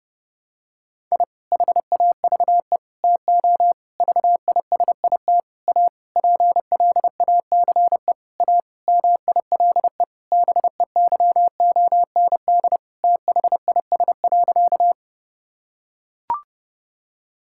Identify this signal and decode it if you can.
Morse code, heard as a keyed tone, about 29 wpm, 700 Hz. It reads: I HAVE TO VISIT A PLACE A MILE BEYOND THIS.
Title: Sentences from Top 1000 Words - Review Part 6 - 30wpm